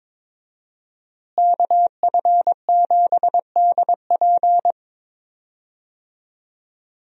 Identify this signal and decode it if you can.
Morse code, heard as a keyed tone, about 22 wpm, 700 Hz. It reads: KF7DP
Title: USA Call Signs - 5 Characters - Repeated 2x - 22wpm